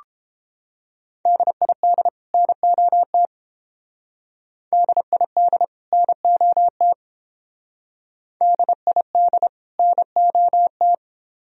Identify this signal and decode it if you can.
Morse code, heard as a keyed tone, about 28 wpm, 700 Hz. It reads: DID NOT DID NOT DID NOT